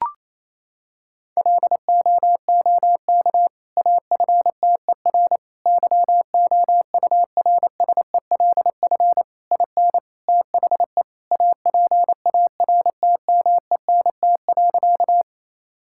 Morse code, heard as a keyed tone, about 28 wpm, 700 Hz. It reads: LOOK AFTER YOURSELF IN THE APARTMENT.
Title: Sentences from Top 1000 Words - Review Part 6 - 28wpm